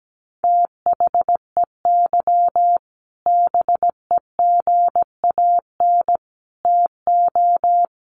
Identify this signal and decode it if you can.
Morse code, heard as a keyed tone, about 17 wpm, 700 Hz. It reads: THEY BEGAN TO
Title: Sentences from Top 1500 Words 17wpm